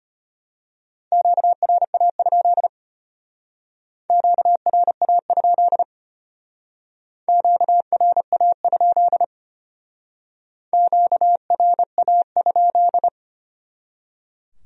Morse code, heard as a keyed tone, about 32 wpm, 700 Hz. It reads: QRA? QRA? QRA? QRA?